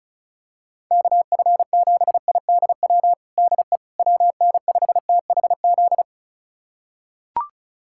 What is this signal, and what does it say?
Morse code, heard as a keyed tone, about 34 wpm, 690 Hz. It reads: KF7IDW DE WN5THZ
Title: USA Callsign DE Callsign - 6 Characters - 35wpm